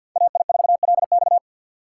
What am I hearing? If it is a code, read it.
Morse code, heard as a keyed tone, about 63 wpm, 700 Hz. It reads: AI4LX